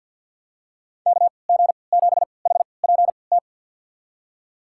Morse code, heard as a keyed tone, about 52 wpm, spaced as for 29 wpm, 700 Hz. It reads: KG7HPT
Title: USA Call Signs - 3x Character Spacing - 50wpm